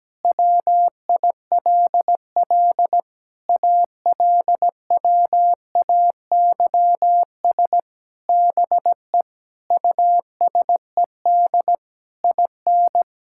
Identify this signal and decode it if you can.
Morse code, heard as a keyed tone, about 17 wpm, 700 Hz. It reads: WILL ALWAYS BE USED IN
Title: Sentences from Top 1000 Words - Review Part 6 - 17wpm